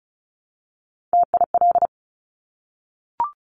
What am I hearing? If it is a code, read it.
Morse code, heard as a keyed tone, about 36 wpm, 700 Hz. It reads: TIL
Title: Top Three Letter Combinations in Words 35wpm